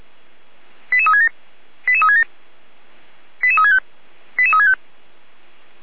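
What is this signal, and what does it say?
Five-tone selective call (ZVEI), heard as quick runs of five tones, tones of about 70 ms. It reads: 88367 88367 88365 88365
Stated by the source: ZVEI five-tone selective call received on SDR.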